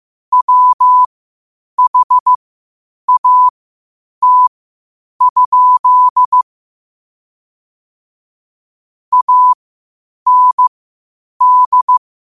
Morse code, heard as a keyed tone, about 15 wpm, spaced as for 8 wpm, 1000 Hz. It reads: WHAT? AND
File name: qotdfiles / 8wpm